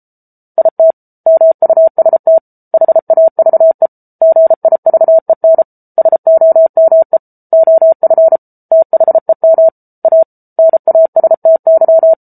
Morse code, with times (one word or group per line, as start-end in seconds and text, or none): 0.6-0.9 IT
1.3-2.4 MUST
2.7-3.9 HAVE
4.2-5.6 GIVEN
6.0-7.2 SOME
7.5-8.4 OF
8.7-9.7 THEM
10.1-10.2 A
10.6-12.1 NASTY